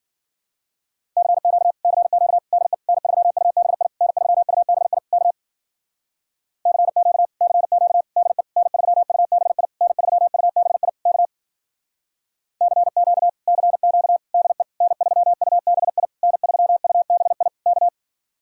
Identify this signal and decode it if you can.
Morse code, heard as a keyed tone, about 54 wpm, 700 Hz. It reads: CQ CQ DE N3UBI N3UBI K CQ CQ DE N3UBI N3UBI K CQ CQ DE N3UBI N3UBI K